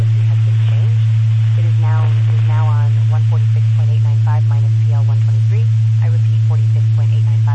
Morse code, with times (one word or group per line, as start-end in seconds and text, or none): none